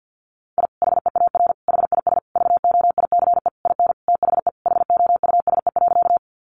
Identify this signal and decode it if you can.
Morse code, heard as a keyed tone, about 50 wpm, 700 Hz. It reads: I HEAR HIS VOICE IN THE HOUSE.
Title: Sentences from Top 500 Words 50wpm